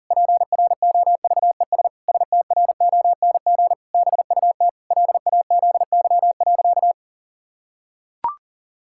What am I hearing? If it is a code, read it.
Morse code, heard as a keyed tone, about 40 wpm, 700 Hz. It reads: PROVES STRONG BUT LAZY.
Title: Pangram Challenge v2 - 40wpm